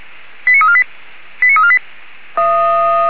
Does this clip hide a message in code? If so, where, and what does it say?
Yes: five-tone selective call (ZVEI) from 0.5 to 1.8 s, heard as quick runs of five tones, tones of about 70 ms. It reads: 89348 89348